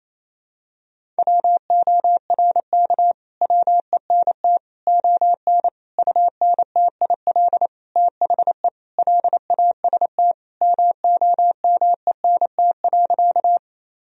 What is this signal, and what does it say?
Morse code, heard as a keyed tone, about 28 wpm, 700 Hz. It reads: WORK WENT ON UNTIL THE LAST MOMENT.
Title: Sentences from Top 1000 Words - Review Part 1 - 28wpm